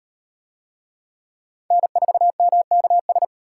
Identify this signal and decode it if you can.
Morse code, heard as a keyed tone, about 37 wpm, 700 Hz. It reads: N4MKS